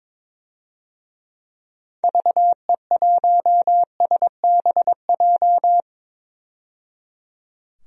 Morse code, heard as a keyed tone, about 22 wpm, 700 Hz. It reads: VE1SBJ
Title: DXCC-100 Call Signs - 6 Characters - 22wpm